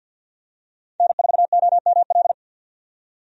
Morse code, heard as a keyed tone, about 50 wpm, 700 Hz. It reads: N4OML
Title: USA Call Signs - 5 Characters - 50wpm